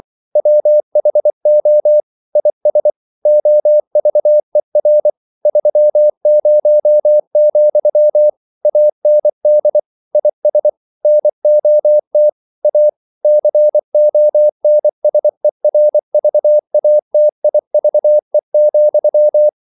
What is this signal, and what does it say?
Morse code, heard as a keyed tone, about 24 wpm, 600 Hz. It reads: WHO IS OVER 30, AND IS NOT A CONSERVATIVE,
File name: CW / Practice / EnglishQuotations0224